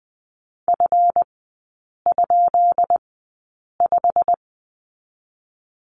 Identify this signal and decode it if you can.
Morse code, heard as a keyed tone, about 20 wpm, spaced as for 8 wpm, 690 Hz. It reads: F?5
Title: 145 Lesson 24e - Letter y - Three Characters 20wpm